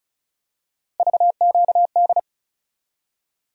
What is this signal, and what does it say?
Morse code, heard as a keyed tone, about 36 wpm, 700 Hz. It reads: VQD
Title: USA Call Sign Postfixes - 3 Letters 35wpm